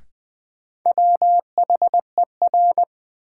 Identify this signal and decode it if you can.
Morse code, heard as a keyed tone, about 20 wpm, 700 Hz. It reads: WHER
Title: Sentences from Top 1500 Words - New Word Drill Alt - 20wpm